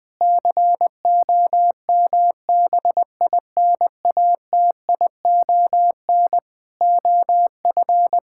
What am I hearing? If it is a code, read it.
Morse code, heard as a keyed tone, about 20 wpm, 700 Hz. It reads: COMBINATION OF
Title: Sentences from Top 1000 Words - Review Part 6 - 20wpm